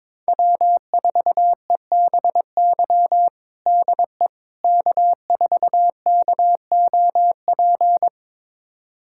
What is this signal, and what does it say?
Morse code, heard as a keyed tone, about 22 wpm, 700 Hz. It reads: W4EBY DE K4KOP
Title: USA Callsign DE Callsign - 5 Characters - 22wpm